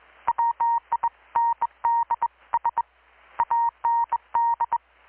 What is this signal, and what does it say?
Morse code, heard as a keyed tone, about 21 wpm, 930 Hz. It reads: WINDS AND
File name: blog / swl / 6-abril-2019-12695-USB-KPS